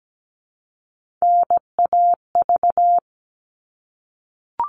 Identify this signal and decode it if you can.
Morse code, heard as a keyed tone, about 17 wpm, 700 Hz. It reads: NAV
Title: All Cut Numbers - 3-Digit 17wpm